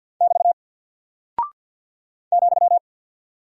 Morse code, heard as a keyed tone, about 48 wpm, 700 Hz. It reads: <BT> ,
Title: Single Character (Letter:Number:Punctuation:Prosign) Alt 50wpm